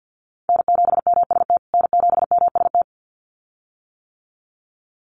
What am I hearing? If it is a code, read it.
Morse code, heard as a keyed tone, about 50 wpm, 700 Hz. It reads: N7MST N7MST
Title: USA Call Signs Repeated 2x 50wpm